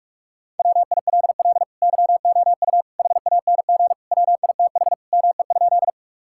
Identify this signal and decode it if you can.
Morse code, heard as a keyed tone, about 47 wpm, 700 Hz. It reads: WILL YOU HANG WITH ME?
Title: Sentences from Top 1600 Words 45wpm